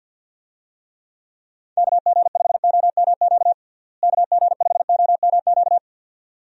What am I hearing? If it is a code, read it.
Morse code, heard as a keyed tone, about 47 wpm, 700 Hz. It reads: KG5OMQ KG5OMQ
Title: USA Call Signs - 6 Characters - Repeated 2x - 50wpm